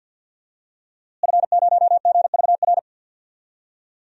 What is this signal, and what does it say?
Morse code, heard as a keyed tone, about 51 wpm, 700 Hz. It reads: F0GVR
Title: DXCC-100 Call Signs - 5 Characters - Repeated 3x - 50wpm